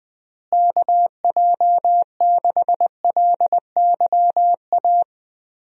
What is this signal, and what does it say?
Morse code, heard as a keyed tone, about 20 wpm, 700 Hz. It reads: KJ6LYA
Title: USA Call Signs - 6 Characters - Repeated 2x - 20wpm